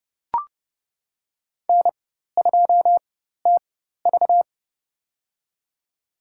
Morse code, heard as a keyed tone, about 30 wpm, spaced as for 14 wpm, 700 Hz. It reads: N2TV
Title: USA Call Signs - 4 Characters - 4x Character Spacing - 30wpm